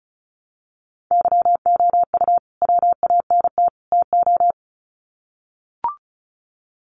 Morse code, heard as a keyed tone, about 34 wpm, 700 Hz. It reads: YOU WANT TO
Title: Sets of 3 Words 35wpm